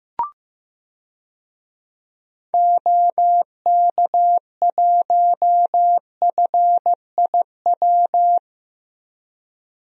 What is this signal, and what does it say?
Morse code, heard as a keyed tone, about 15 wpm, 700 Hz. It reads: OK1FIW
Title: DXCC-100 Call Signs - 6 Characters - 15wpm